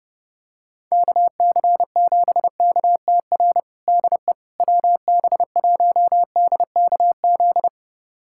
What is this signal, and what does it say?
Morse code, heard as a keyed tone, about 30 wpm, 700 Hz. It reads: KC7KTR DE WB1DKZ
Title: USA Callsign DE Callsign 30wpm